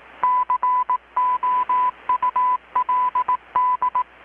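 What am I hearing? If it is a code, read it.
Morse code, heard as a keyed tone, about 18 wpm, 1000 Hz. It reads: COULD
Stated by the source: Morse from KPH's Night of Nights XX maritime radiotelegraphy event, received in Chiapas, Mexico about 3,800 km away on a Kenwood R-600 with a grounded 10 m long-wire antenna.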